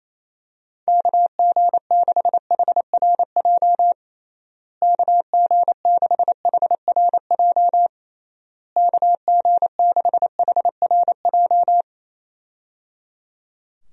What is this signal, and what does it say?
Morse code, heard as a keyed tone, about 28 wpm, 700 Hz. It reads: KG6HRJ KG6HRJ KG6HRJ
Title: USA Call Signs - 6 Characters - Repeated 3x - 28wpm